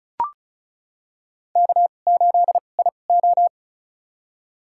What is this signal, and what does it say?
Morse code, heard as a keyed tone, about 36 wpm, spaced as for 25 wpm, 700 Hz. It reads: K8IO
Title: USA Call Signs - 4 Characters - 2x Character Spacing - 35wpm